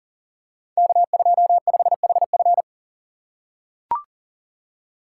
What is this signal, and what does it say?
Morse code, heard as a keyed tone, about 42 wpm, 700 Hz. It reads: K25HF
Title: Code Groups - 5 Letter:Numbers - 40wpm